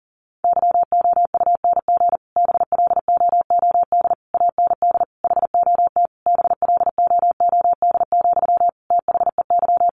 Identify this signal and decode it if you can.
Morse code, heard as a keyed tone, about 40 wpm, 700 Hz. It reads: YOUNG BLOOD AND HOT BLOOD, THEY
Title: Sentences from Top 1100 Words 40wpm